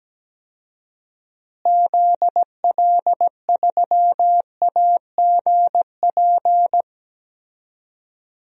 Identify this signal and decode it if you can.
Morse code, heard as a keyed tone, about 17 wpm, 700 Hz. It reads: ZL3AGP
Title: DXCC-100 Call Signs - 6 Characters - Repeated 2x - 17wpm